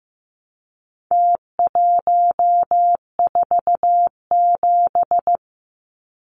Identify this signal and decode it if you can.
Morse code, heard as a keyed tone, about 15 wpm, 700 Hz. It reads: T147
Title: Numbers - Contest Style - 4-Digit 15wpm